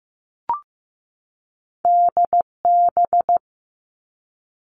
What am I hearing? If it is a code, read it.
Morse code, heard as a keyed tone, about 15 wpm, 700 Hz. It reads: DB
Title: Two Challenging Letter-Number Combinations 15wpm